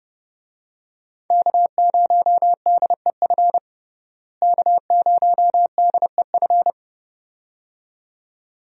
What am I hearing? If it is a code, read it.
Morse code, heard as a keyed tone, about 30 wpm, 700 Hz. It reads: K0DEF K0DEF
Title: USA Call Signs - 5 Characters - Repeated 2x - 30wpm